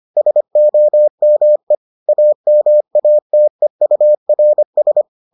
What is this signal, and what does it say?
Morse code, heard as a keyed tone, about 25 wpm, 600 Hz. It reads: SOME AMATEURS